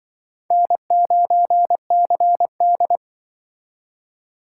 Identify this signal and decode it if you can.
Morse code, heard as a keyed tone, about 25 wpm, 700 Hz. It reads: N9CD
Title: Speed Racing - USA 1x2 Call Signs 20wpm